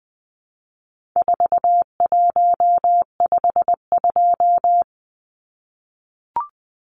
Morse code, heard as a keyed tone, about 20 wpm, 700 Hz. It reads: 4152
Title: Numbers - 4-Digit 20wpm